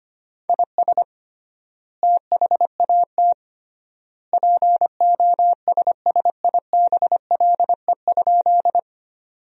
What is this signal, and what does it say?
Morse code, heard as a keyed tone, about 25 wpm, 700 Hz. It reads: IS THAT POSSIBLE?
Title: 3x Word-Spacing - Sentences from Top 500 Words - 25wpm